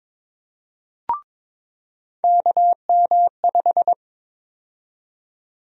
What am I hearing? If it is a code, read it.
Morse code, heard as a keyed tone, about 21 wpm, 700 Hz. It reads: KM5
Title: USA Call Sign Prefixes - 22wpm